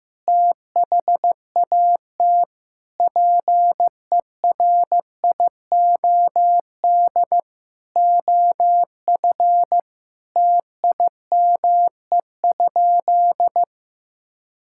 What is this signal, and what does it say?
Morse code, heard as a keyed tone, about 15 wpm, 700 Hz. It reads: THAT PERIOD OF TIME?
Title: Sentences from Top 1000 Words - Review Part 5 - 15wpm